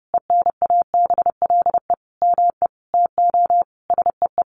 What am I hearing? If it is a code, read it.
Morse code, heard as a keyed tone, about 30 wpm, 700 Hz. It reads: ENABLE ME TO SEE
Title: Sentences from Top 1500 Words 30wpm